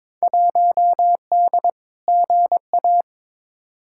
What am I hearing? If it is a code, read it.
Morse code, heard as a keyed tone, about 22 wpm, 700 Hz. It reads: 1D GA
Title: Field Day Exchanges - 22wpm